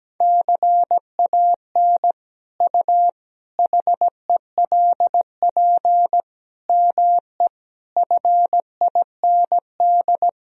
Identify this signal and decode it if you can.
Morse code, heard as a keyed tone, about 17 wpm, 700 Hz. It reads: CAN U HELP ME FIND
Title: Sentences from Top 200 Words and Abbreviations 17wpm